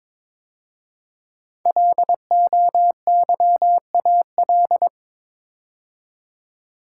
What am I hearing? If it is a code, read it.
Morse code, heard as a keyed tone, about 23 wpm, 700 Hz. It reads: LOYAL
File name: 5 Letter Words - First 3 Letters Given - 22wpm